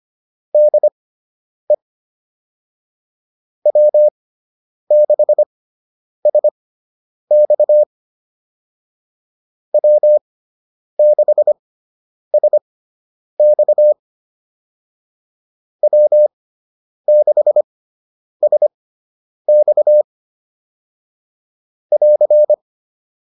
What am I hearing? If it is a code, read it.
Morse code, heard as a keyed tone, about 25 wpm, spaced as for 9 wpm, 600 Hz. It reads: DE W6SX W6SX W6SX <AR>